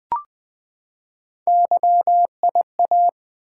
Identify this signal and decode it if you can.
Morse code, heard as a keyed tone, about 20 wpm, 700 Hz. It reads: YIA
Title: Three Letter-Number Combinations - 20wpm